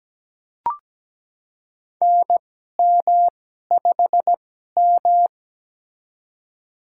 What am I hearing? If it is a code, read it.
Morse code, heard as a keyed tone, about 17 wpm, spaced as for 12 wpm, 700 Hz. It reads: NM5M
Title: USA Call Signs - 4 Characters - 2x Character Spacing - 17wpm